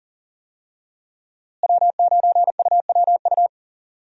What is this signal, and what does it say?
Morse code, heard as a keyed tone, about 40 wpm, 700 Hz. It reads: W9UWU
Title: USA Call Signs - 5 Characters - Repeated 2x - 40wpm